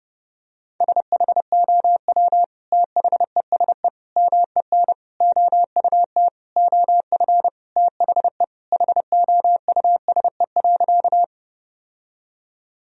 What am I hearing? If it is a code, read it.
Morse code, heard as a keyed tone, about 30 wpm, 700 Hz. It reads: SHOW THESE MEN OUT OF THE HOUSE.
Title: Sentences from Top 1000 Words - Review Part 1 - 30wpm